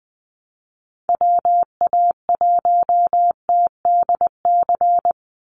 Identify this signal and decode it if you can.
Morse code, heard as a keyed tone, about 20 wpm, 700 Hz. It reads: WA1TDC
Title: USA Call Signs 20wpm